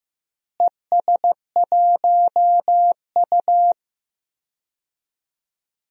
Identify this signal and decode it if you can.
Morse code, heard as a keyed tone, about 15 wpm, 700 Hz. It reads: ES1U
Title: DXCC-100 Call Signs - 4 Characters - Repeated 2x - 15wpm